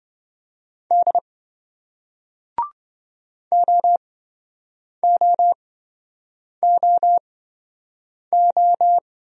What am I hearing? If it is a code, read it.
Morse code, heard as a keyed tone, about 28 wpm, spaced as for 7 wpm, 700 Hz. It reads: D OOOO